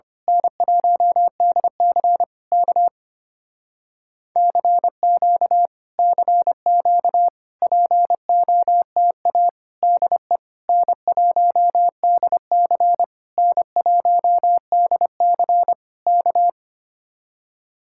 Morse code, heard as a keyed tone, about 24 wpm, 700 Hz. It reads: N1DC K CQ CQ POTA DE N1DC N1DC K
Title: Speed-Racing Common QSO Phrases Encore - 25wpm